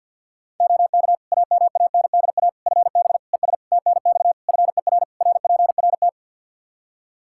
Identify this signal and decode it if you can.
Morse code, heard as a keyed tone, about 52 wpm, 700 Hz. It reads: OK AMANDA FB ES TNX FER RPRT
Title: K1USN SST QSO Practice 50wpm